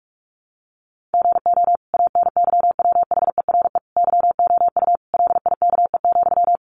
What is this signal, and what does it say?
Morse code, heard as a keyed tone, about 44 wpm, 700 Hz. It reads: GO ANYWHERE YOU LIKE,
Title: Sentences from Top 1400 Words 45wpm